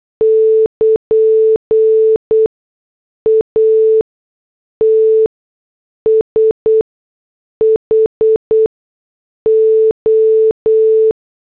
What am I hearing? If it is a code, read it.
Morse code, heard as a keyed tone, about 8 wpm, spaced as for 6 wpm, 440 Hz. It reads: <KN> ATSHO
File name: morse / headlines